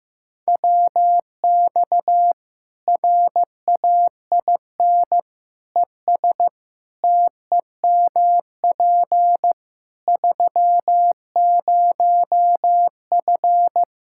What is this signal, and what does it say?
Morse code, heard as a keyed tone, about 15 wpm, 700 Hz. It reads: WX RAIN ES TEMP 30F
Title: K1USN SST QSO Practice 15wpm